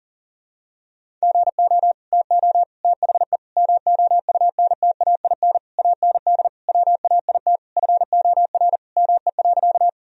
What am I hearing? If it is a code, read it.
Morse code, heard as a keyed tone, about 40 wpm, 700 Hz. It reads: GO TO THE MOUNTAIN AND WAIT FOR ME.
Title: Sentences from Top 1000 Words - Review Part 1 - 40wpm